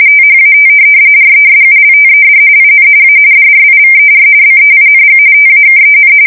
RTTY, about 45.45 baud. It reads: THAT RESULTED IN A DRIVER BEING TAKEN
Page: 170 Shift 2125/2295